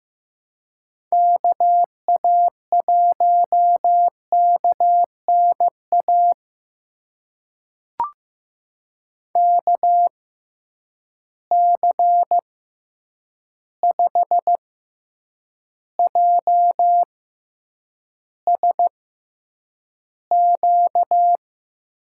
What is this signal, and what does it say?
Morse code, heard as a keyed tone, about 15 wpm, spaced as for 8 wpm, 700 Hz. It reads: KA1KNA KC5JSQ
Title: USA Call Signs - 6 Characters - 6x Character Spacing - 15wpm